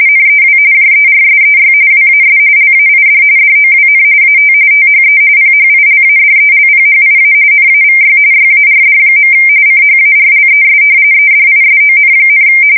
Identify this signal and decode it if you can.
RTTY, about 45.45 baud. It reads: NEW JERSEY WAS LAID DOWN AT THE PHILADELPHIA NAVY YARD. CHARLES EDISON,
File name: fsk170 / battleshipnj